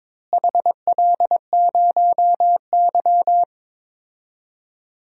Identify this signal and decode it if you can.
Morse code, heard as a keyed tone, about 22 wpm, 700 Hz. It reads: HL0Y
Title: DXCC-100 Call Signs - 4 Characters - Repeated 2x - 22wpm